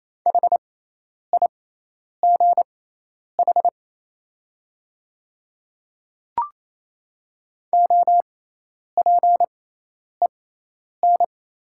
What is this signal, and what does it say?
Morse code, heard as a keyed tone, about 27 wpm, spaced as for 10 wpm, 690 Hz. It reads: HIGH OPEN
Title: 6x Character Spacing - Top 300 Words - 28wpm